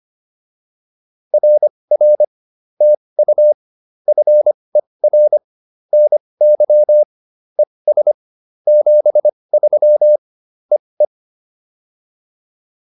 Morse code, heard as a keyed tone, about 25 wpm, spaced as for 20 wpm, 600 Hz. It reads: RR TU FER NY ES 73 EE
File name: POTA206 20